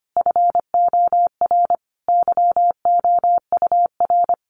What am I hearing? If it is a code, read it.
Morse code, heard as a keyed tone, about 25 wpm, 700 Hz. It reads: FOR YOUR
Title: A Sound Of Thunder 25wpm